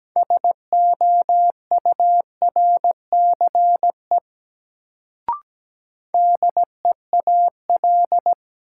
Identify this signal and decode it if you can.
Morse code, heard as a keyed tone, about 17 wpm, 700 Hz. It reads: SOURCE DEAL
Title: Sentences from Top 800 Words - New Word Drill - 17wpm